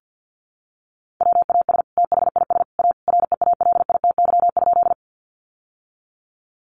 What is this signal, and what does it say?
Morse code, heard as a keyed tone, about 50 wpm, 700 Hz. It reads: WAS THIS A REALITY?
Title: Sentences from Top 1200 Words 50wpm